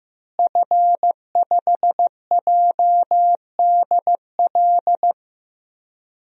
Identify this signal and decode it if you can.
Morse code, heard as a keyed tone, about 15 wpm, 700 Hz. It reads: F5JDL
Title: DXCC-100 Call Signs - 5 Characters - Repeated 2x - 15wpm